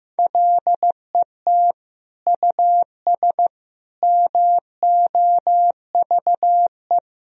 Morse code, heard as a keyed tone, about 15 wpm, 700 Hz. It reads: LET US MOVE
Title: Sentences from Top 1000 Words - Review Part 2 - 15wpm